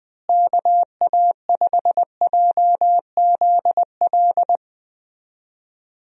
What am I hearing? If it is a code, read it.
Morse code, heard as a keyed tone, about 20 wpm, 700 Hz. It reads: KA5JZL
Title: USA Call Signs - 6 Characters - Repeated 2x - 20wpm